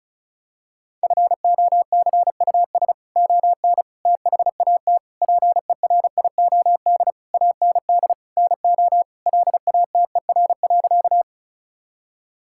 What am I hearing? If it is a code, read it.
Morse code, heard as a keyed tone, about 36 wpm, 700 Hz. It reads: FOCUS ON THAT PERIOD AND NO LATER.
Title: Sentences from Top 1000 Words - Review Part 5 - 35wpm